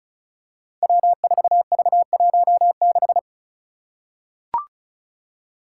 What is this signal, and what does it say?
Morse code, heard as a keyed tone, about 34 wpm, 700 Hz. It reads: W4V16
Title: Five Challenging Letter-Number Combinations 35wpm